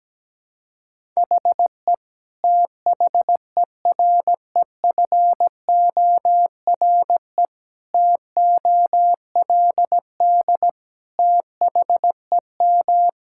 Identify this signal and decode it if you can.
Morse code, heard as a keyed tone, about 17 wpm, 700 Hz. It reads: HE THEREFORE TOLD THEM
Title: Sentences from Top 1000 Words - Review Part 5 - 17wpm